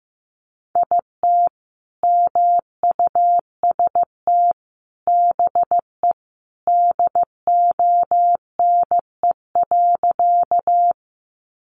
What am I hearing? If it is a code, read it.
Morse code, heard as a keyed tone, about 15 wpm, 700 Hz. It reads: IT MUST BE DONE.
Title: Sentences from Top 500 Words 15wpm